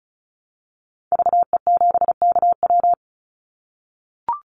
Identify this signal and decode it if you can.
Morse code, heard as a keyed tone, about 34 wpm, 700 Hz. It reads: VE7KW
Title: Call Signs - 35wpm